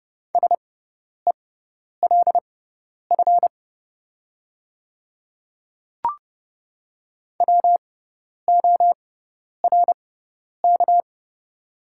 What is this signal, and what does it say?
Morse code, heard as a keyed tone, about 30 wpm, spaced as for 10 wpm, 700 Hz. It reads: SELF WORK
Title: 6x Character Spacing - Top 300 Words - 30wpm